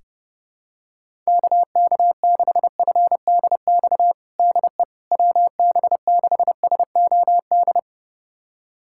Morse code, heard as a keyed tone, about 32 wpm, 700 Hz. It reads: KK6FDX DE WB6SOD
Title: USA Callsign DE Callsign - 6 Characters - 30wpm